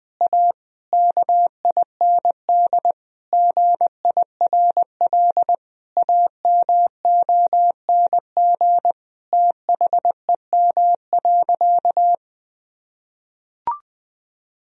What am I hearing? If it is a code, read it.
Morse code, heard as a keyed tone, about 20 wpm, 700 Hz. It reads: A KIND GIRL AMONG THEM.
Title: Sentences from Top 1000 Words - Review Part 3 - 20wpm